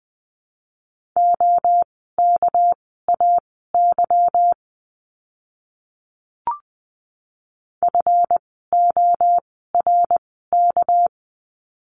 Morse code, heard as a keyed tone, about 20 wpm, spaced as for 15 wpm, 700 Hz. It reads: OKAY FORK
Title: 199 Lesson 32f - Letter k - Words 20wpm